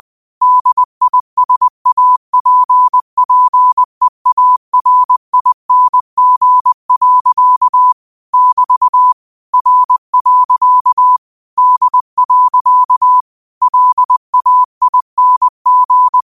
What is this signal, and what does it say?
Morse code, heard as a keyed tone, about 20 wpm, 1000 Hz. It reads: DISAPPEARING. <BT> R. D. LAING
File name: qotdfiles / 20wpm